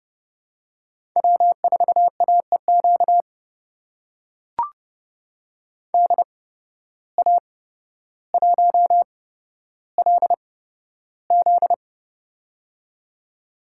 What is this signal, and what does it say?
Morse code, heard as a keyed tone, about 29 wpm, spaced as for 13 wpm, 700 Hz. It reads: W4AEQ DA1LZ
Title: USA Call Signs - 8x Character Spacing - 30wpm